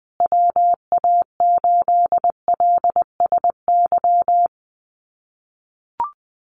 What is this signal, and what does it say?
Morse code, heard as a keyed tone, about 20 wpm, 700 Hz. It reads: WA8LSY
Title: USA Call Signs Repeated 3x 20wpm